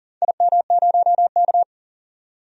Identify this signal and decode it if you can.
Morse code, heard as a keyed tone, about 43 wpm, 700 Hz. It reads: IM0K
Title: DXCC-100 Call Signs - 4 Characters - 40wpm